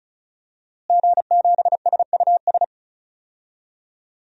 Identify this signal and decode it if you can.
Morse code, heard as a keyed tone, about 36 wpm, 700 Hz. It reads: G7SUS
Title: DXCC-100 Call Signs - 5 Characters - Repeated 3x - 35wpm